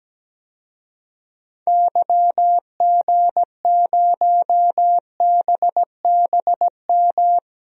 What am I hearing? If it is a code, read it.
Morse code, heard as a keyed tone, about 17 wpm, 700 Hz. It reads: YG0BBM
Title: DXCC-100 Call Signs - 6 Characters - Repeated 3x - 17wpm